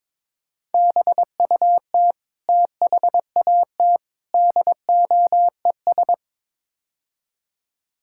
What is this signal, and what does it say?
Morse code, heard as a keyed tone, about 22 wpm, 700 Hz. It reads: BUT THAT DOES
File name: practice / 22wpm / Build A Phrase - 5-Word Phrases - 22wpm